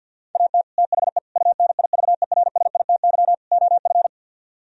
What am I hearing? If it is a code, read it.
Morse code, heard as a keyed tone, about 63 wpm, 700 Hz. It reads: AT THE UNIVERSITY OF